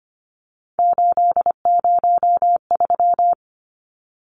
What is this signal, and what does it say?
Morse code, heard as a keyed tone, about 25 wpm, 700 Hz. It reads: 803
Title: Numbers - 3-Digit 25wpm